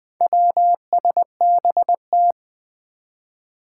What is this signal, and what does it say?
Morse code, heard as a keyed tone, about 21 wpm, 700 Hz. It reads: WSBT
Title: Four Challenging Letter-Number Combinations 20wpm